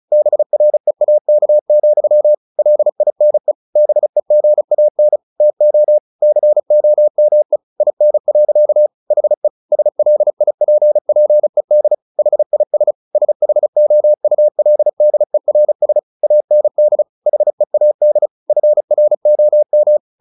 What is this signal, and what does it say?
Morse code, heard as a keyed tone, about 35 wpm, 600 Hz. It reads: BREAK, LINE BEGAN TO COME IN. HE SLIPPED HIS SHOULDERS AND HEAD FROM
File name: the old man and the sea 2 35wpm